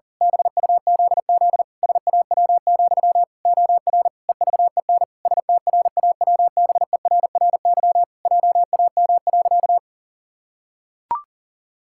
Morse code, heard as a keyed tone, about 40 wpm, 700 Hz. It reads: BUZZ SAW, OR EVEN STRAWBERRY JAM.
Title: Pangram Challenge v2 - 40wpm